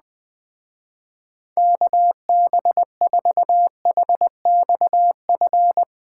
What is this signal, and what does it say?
Morse code, heard as a keyed tone, about 21 wpm, 700 Hz. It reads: KB4HXF